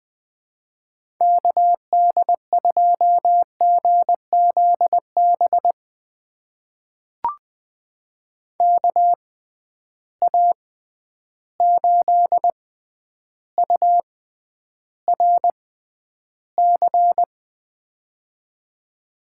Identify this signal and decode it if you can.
Morse code, heard as a keyed tone, about 20 wpm, spaced as for 10 wpm, 700 Hz. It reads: KD2GZB KA8URC
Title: USA Call Signs - 6 Characters - 6x Character Spacing - 20wpm